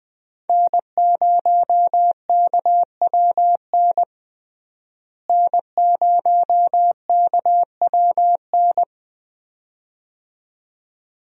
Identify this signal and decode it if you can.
Morse code, heard as a keyed tone, about 20 wpm, 700 Hz. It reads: N0KWN N0KWN
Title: USA Call Signs - 5 Characters - Repeated 2x - 20wpm